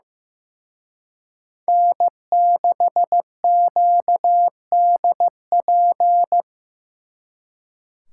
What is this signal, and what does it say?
Morse code, heard as a keyed tone, about 18 wpm, 700 Hz. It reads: N6QDP
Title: Speed Racing - USA 1x3 Call Signs 15wpm